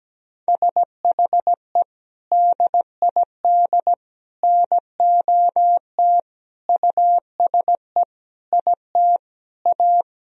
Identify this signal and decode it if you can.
Morse code, heard as a keyed tone, about 17 wpm, 700 Hz. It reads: SHE DID NOT USE IT A
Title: Sentences from Top 1000 Words - Review Part 1 - 17wpm